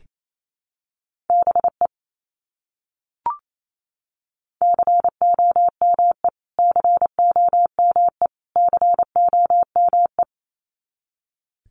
Morse code, heard as a keyed tone, about 28 wpm, 700 Hz. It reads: BE COME COME COME
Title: Top 200 Words Repeated 28wpm